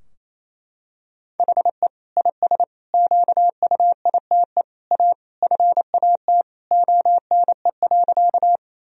Morse code, heard as a keyed tone, about 28 wpm, 700 Hz. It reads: HE IS QUITE A FAT ONE.
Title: Sentences from Top 1000 Words - Review Part 4 - 28wpm